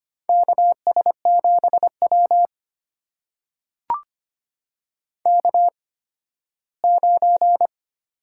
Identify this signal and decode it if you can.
Morse code, heard as a keyed tone, about 25 wpm, spaced as for 15 wpm, 700 Hz. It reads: KS7W K9
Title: USA Call Signs - 4 Characters - 8x Character Spacing - 25wpm